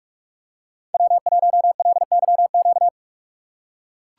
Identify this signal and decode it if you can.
Morse code, heard as a keyed tone, about 42 wpm, 700 Hz. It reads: W1LYQ
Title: USA Call Signs - 5 Characters - 4x Character Spacing - 45wpm